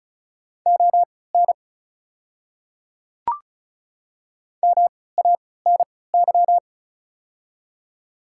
Morse code, heard as a keyed tone, about 36 wpm, spaced as for 20 wpm, 700 Hz. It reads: ON MANY
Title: 3x Character Spacing - Top 100 Words - 35wpm